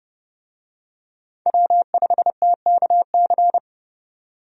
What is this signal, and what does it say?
Morse code, heard as a keyed tone, about 30 wpm, 700 Hz. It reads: W5TKC
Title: USA Call Signs - 5 Characters - 30wpm